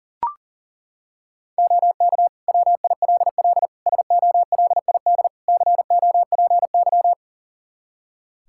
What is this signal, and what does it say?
Morse code, heard as a keyed tone, about 40 wpm, 700 Hz. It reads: OK WILL SOLID COPY
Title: Practice QSO - Exchange 3 - More Info - 40wpm